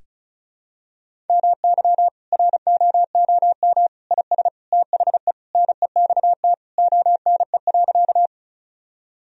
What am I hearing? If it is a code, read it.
Morse code, heard as a keyed tone, about 36 wpm, 700 Hz. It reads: MY ROOM IS THE NEXT ONE.
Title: Sentences from Top 1000 Words - Review Part 1 - 35wpm